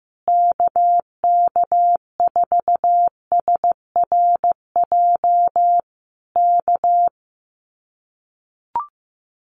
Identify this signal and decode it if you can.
Morse code, heard as a keyed tone, about 15 wpm, 700 Hz. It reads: KK4SRJ K
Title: Common QSO Phrases Encore 15wpm